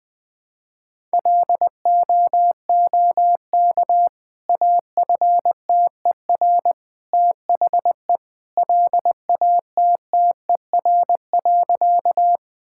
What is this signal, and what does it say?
Morse code, heard as a keyed tone, about 20 wpm, 700 Hz. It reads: LOOK AFTER THE LATTER.
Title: Sentences from Top 1600 Words 20wpm